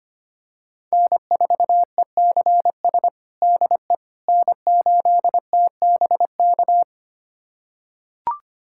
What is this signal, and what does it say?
Morse code, heard as a keyed tone, about 25 wpm, 700 Hz. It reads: N4ECS DE N8TBK
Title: USA Callsign DE Callsign - 5 Characters - 25wpm